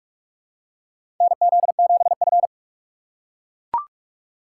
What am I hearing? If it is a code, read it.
Morse code, heard as a keyed tone, about 43 wpm, 700 Hz. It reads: NZ7F
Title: USA Call Signs - 4 Characters - 3x Character Spacing - 45wpm